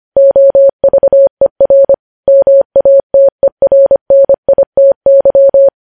Morse code, heard as a keyed tone, about 25 wpm, 550 Hz. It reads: OVER MATERNITY
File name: podcast / News headlines-1x-25x25